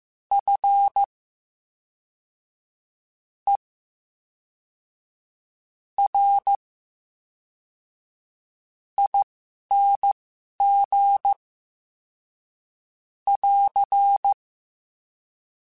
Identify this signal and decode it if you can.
Morse code, heard as a keyed tone, about 15 wpm, spaced as for 5 wpm, 800 Hz. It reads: FERING <AR>
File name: Special INGs / Special INGs-mp3s / VerySlowToFaster Ing @10